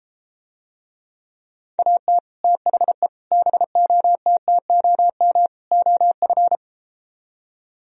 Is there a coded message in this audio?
Yes: Morse code from 1.8 to 6.6 s, heard as a keyed tone, about 33 wpm, 700 Hz. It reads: AT THE BOTTOM OF